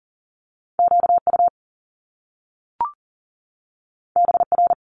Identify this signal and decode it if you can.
Morse code, heard as a keyed tone, about 40 wpm, 700 Hz. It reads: QU BR
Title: Top Two Letter Combinations in Words 40wpm